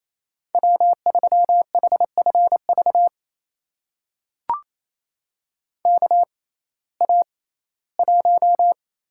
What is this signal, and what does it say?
Morse code, heard as a keyed tone, about 28 wpm, spaced as for 17 wpm, 700 Hz. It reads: W3HFV KA1
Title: USA Call Signs - 5 Characters - 6x Character Spacing - 28wpm